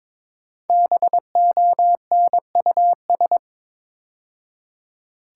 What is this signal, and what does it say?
Morse code, heard as a keyed tone, about 22 wpm, 700 Hz. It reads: BONUS
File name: 5 Letter Words - First 3 Letters Given - 22wpm